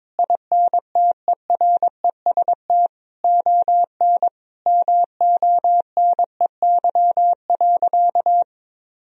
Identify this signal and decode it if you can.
Morse code, heard as a keyed tone, about 22 wpm, 700 Hz. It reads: INTEREST ON MONEY.
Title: Sentences from Top 1000 Words - Review Part 4 - 22wpm